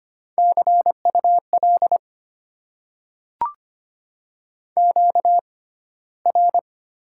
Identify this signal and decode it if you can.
Morse code, heard as a keyed tone, about 25 wpm, spaced as for 15 wpm, 700 Hz. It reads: CUL QR
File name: Top 90 QSO Elements - 6x Character Spacing - 25wpm